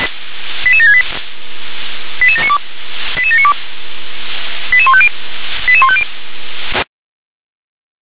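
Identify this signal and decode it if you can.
Five-tone selective call (ZVEI), heard as quick runs of five tones, tones of about 70 ms. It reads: 88768 88792 88792 88150 88150
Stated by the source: Five-tone ZVEI selective call 88655, received on an SDR.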